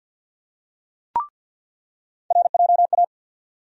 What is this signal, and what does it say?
Morse code, heard as a keyed tone, about 53 wpm, 700 Hz. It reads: RJA
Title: USA Call Sign Postfixes - 3 Letters 50wpm